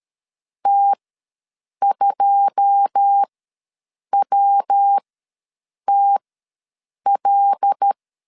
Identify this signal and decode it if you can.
Morse code, heard as a keyed tone, about 13 wpm, spaced as for 7 wpm, 800 Hz. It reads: T2WTL
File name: LEZIONI / 9 2 ws 12wpm 3406